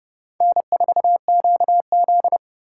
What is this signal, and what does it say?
Morse code, heard as a keyed tone, about 29 wpm, 700 Hz. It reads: N4QZ